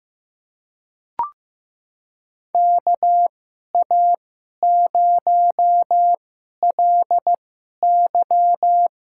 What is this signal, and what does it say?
Morse code, heard as a keyed tone, about 15 wpm, spaced as for 11 wpm, 700 Hz. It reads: KA0LY
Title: USA Call Signs - 2x Character Spacing - 15wpm